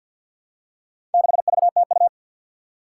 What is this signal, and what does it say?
Morse code, heard as a keyed tone, about 52 wpm, 700 Hz. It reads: 6VTU
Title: Four Challenging Letter-Number Combinations 50wpm